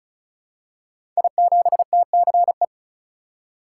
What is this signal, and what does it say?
Morse code, heard as a keyed tone, about 36 wpm, 700 Hz. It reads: I7TCE
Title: Code Groups - 5 Letter:Numbers - 35wpm